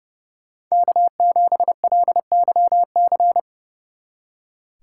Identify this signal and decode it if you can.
Morse code, heard as a keyed tone, about 37 wpm, 700 Hz. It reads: K7LYC
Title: Speed Racing - USA 1x3 Call Signs 30wpm